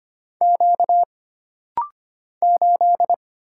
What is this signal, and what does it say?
Morse code, heard as a keyed tone, about 25 wpm, 700 Hz. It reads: Q 8
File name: Single Character (Letter:Number:Punctuation:Prosign) - Rapid-Fire - 25wpm